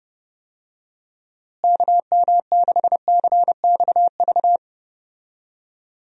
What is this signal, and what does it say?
Morse code, heard as a keyed tone, about 30 wpm, 700 Hz. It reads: KM6CXV
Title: USA Call Signs - 6 Characters - Repeated 3x - 30wpm